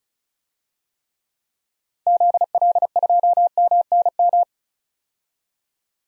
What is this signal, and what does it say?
Morse code, heard as a keyed tone, about 34 wpm, 700 Hz. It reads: ZL2MNM
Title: DXCC-100 Call Signs - 6 Characters - 35wpm